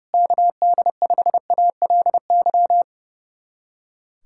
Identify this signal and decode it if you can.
Morse code, heard as a keyed tone, about 30 wpm, 700 Hz. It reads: KD5ALY
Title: USA Callsign DE Callsign 30wpm